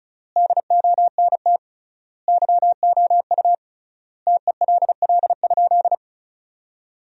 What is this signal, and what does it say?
Morse code, heard as a keyed tone, about 36 wpm, 700 Hz. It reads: DONT YOU TELL?
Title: 3x Word-Spacing - Sentences from Top 300 Words - 35wpm